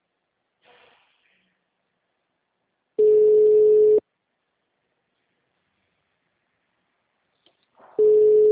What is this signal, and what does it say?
Ringback tone (Europe, 425 Hz).
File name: out- -2003- - - .45787